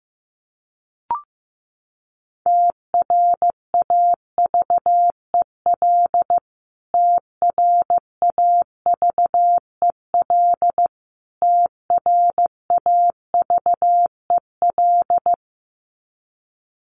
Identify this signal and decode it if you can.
Morse code, heard as a keyed tone, about 15 wpm, 700 Hz. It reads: TRAVEL TRAVEL TRAVEL
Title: Top 500 Words Repeated 15wpm